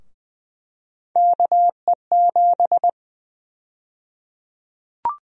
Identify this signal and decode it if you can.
Morse code, heard as a keyed tone, about 21 wpm, 700 Hz. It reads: KE7
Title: USA Call Sign Prefixes - Repeated 2x - 20wpm